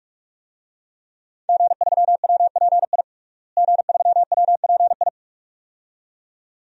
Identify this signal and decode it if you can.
Morse code, heard as a keyed tone, about 46 wpm, 700 Hz. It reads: G3WPI G3WPI
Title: DXCC-100 Call Signs - 5 Characters - Repeated 2x - 45wpm